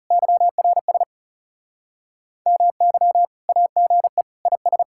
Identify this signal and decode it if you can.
Morse code, heard as a keyed tone, about 36 wpm, 700 Hz. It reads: YRS MY AGE IS
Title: Speed-Racing Common QSO Phrases - 30wpm